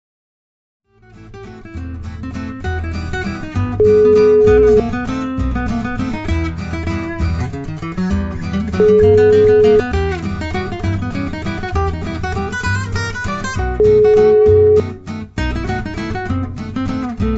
Ringback tone (Europe, 425 Hz), with noise.